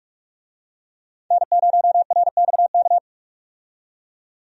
Morse code, heard as a keyed tone, about 45 wpm, 700 Hz. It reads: N0RXK
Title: USA Call Signs - 5 Characters - 45wpm